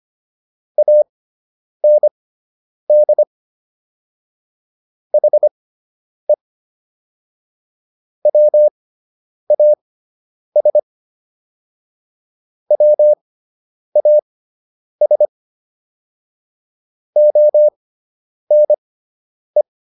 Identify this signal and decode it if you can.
Morse code, heard as a keyed tone, about 25 wpm, spaced as for 9 wpm, 600 Hz. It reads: AND HE WAS WAS ONE